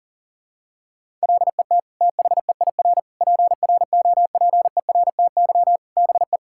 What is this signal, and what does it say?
Morse code, heard as a keyed tone, about 37 wpm, 700 Hz. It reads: LET THEIR PROPERTY BE
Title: Sentences from Top 1600 Words 40wpm